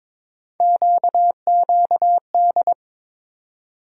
Morse code, heard as a keyed tone, about 22 wpm, 700 Hz. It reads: QQD
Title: USA Call Sign Postfixes - 3 Letters 22wpm